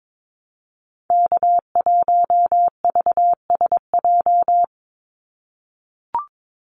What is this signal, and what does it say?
Morse code, heard as a keyed tone, about 22 wpm, 700 Hz. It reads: K1VSJ
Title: Call Signs - 22wpm